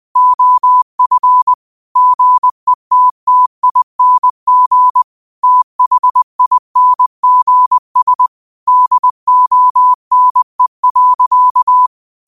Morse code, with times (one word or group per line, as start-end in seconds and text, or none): 0.2-1.5 OF
2.0-5.0 GETTING
5.4-8.3 THINGS
8.7-11.9 DONE.